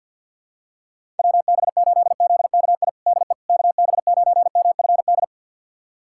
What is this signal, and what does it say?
Morse code, heard as a keyed tone, about 46 wpm, 700 Hz. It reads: WB8ZKI DE KB9MFD
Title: USA Callsign DE Callsign - 6 Characters - 50wpm